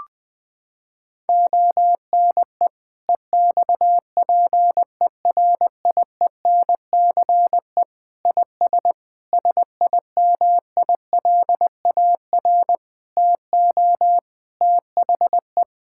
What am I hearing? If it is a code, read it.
Morse code, heard as a keyed tone, about 21 wpm, 700 Hz. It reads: ONE EXPERIENCE IS SIMILAR TO THE
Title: Sentences from Top 1000 Words - Review Part 5 - 20wpm